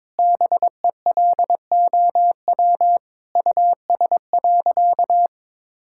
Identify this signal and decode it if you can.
Morse code, heard as a keyed tone, about 22 wpm, 700 Hz. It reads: BELOW US.
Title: Sentences from Top 1000 Words - Review Part 5 - 22wpm